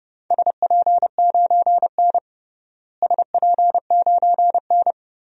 Morse code, heard as a keyed tone, about 32 wpm, 700 Hz. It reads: SP9N SP9N
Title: DXCC-100 Call Signs - 4 Characters - Repeated 2x - 30wpm